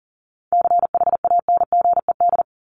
Morse code, heard as a keyed tone, about 40 wpm, 700 Hz. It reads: CHANGED
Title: ED Suffix - Top 100 Words with Base Words 40wpm